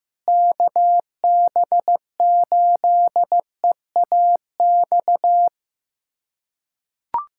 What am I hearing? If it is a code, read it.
Morse code, heard as a keyed tone, about 15 wpm, 700 Hz. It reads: KB8EAX
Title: USA Callsign DE Callsign - 6 Characters - 15wpm